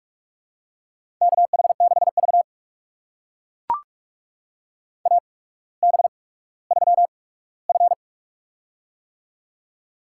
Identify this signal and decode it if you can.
Morse code, heard as a keyed tone, about 44 wpm, spaced as for 19 wpm, 700 Hz. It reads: KH6V AB3F
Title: USA Call Signs - 4 Characters - 8x Character Spacing - 45wpm